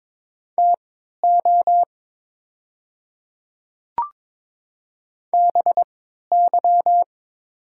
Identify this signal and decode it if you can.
Morse code, heard as a keyed tone, about 22 wpm, spaced as for 12 wpm, 700 Hz. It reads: TO BY
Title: 3x Character Spacing - Top 100 Words - 22wpm